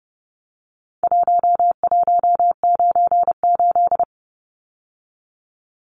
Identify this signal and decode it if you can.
Morse code, heard as a keyed tone, about 30 wpm, 700 Hz. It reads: 1198
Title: Numbers - Contest Style - 4-Digit 30wpm